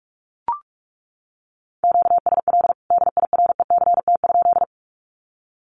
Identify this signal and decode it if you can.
Morse code, heard as a keyed tone, about 44 wpm, 700 Hz. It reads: QSL DIRECT?
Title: Common QSO Phrases 45wpm